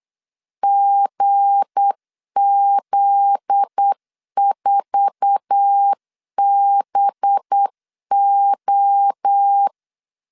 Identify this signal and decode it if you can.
Morse code, heard as a keyed tone, about 8 wpm, 800 Hz. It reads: GZ4BO
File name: LEZIONI / 8 YJ45 / 8 YJ4 8wpm 3246